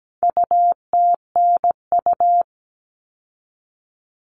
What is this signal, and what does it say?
Morse code, heard as a keyed tone, about 17 wpm, 700 Hz. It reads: UTNU
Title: All Cut Numbers - 4-Digit 17wpm